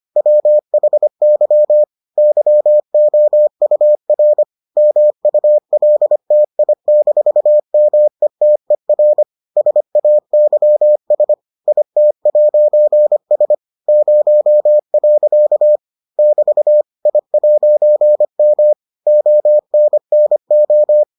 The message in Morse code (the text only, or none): WHY YOUR MULTI-METER SAYS IT'S 0. <BT> I'M ONNO